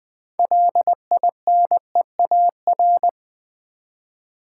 Morse code, heard as a keyed tone, about 20 wpm, 700 Hz. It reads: LINEAR
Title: 6 Letter Words - First 3 Letters Given - 20wpm